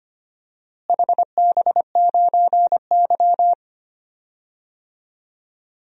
Morse code, heard as a keyed tone, about 25 wpm, 700 Hz. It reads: HB9Y
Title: DXCC-100 Call Signs - 4 Characters - Repeated 3x - 25wpm